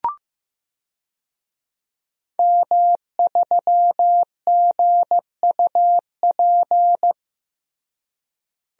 Morse code, heard as a keyed tone, about 15 wpm, 700 Hz. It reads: M3GUP
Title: DXCC-100 Call Signs - 5 Characters - Repeated 2x - 15wpm